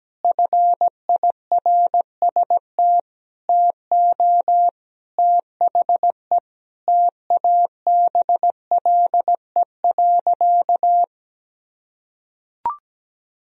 Morse code, heard as a keyed tone, about 17 wpm, 700 Hz. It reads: FIRST TO THE TABLE.
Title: Sentences from Top 1000 Words - Review Part 2 - 17wpm